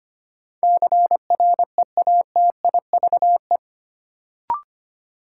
Morse code, heard as a keyed tone, about 25 wpm, 700 Hz. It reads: CREATIVE
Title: Sentences from Top 1000 Words - Review Part 5 - New Word Drill - 25wpm